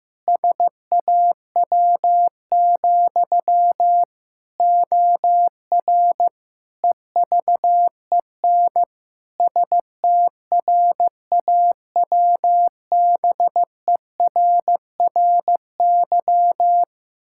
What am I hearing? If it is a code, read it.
Morse code, heard as a keyed tone, about 15 wpm, 700 Hz. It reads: SAW, OR EVEN STRAWBERRY
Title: Pangram Challenge v2 - 15wpm